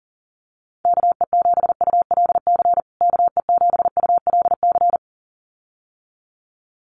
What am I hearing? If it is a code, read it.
Morse code, heard as a keyed tone, about 37 wpm, 700 Hz. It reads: KE7ULC KE7ULC
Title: USA Call Signs Repeated 2x 40wpm